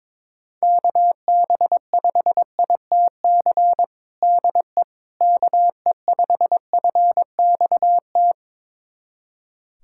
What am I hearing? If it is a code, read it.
Morse code, heard as a keyed tone, about 22 wpm, 700 Hz. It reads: KB5ITC DE KE5FXT
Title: USA Callsign DE Callsign - 6 Characters - 22wpm